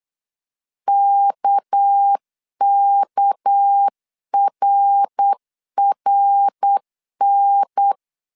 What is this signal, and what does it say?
Morse code, heard as a keyed tone, about 8 wpm, 800 Hz. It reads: KKRRN
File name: CW / DET / 1 6 1500C S C 8wpm 5219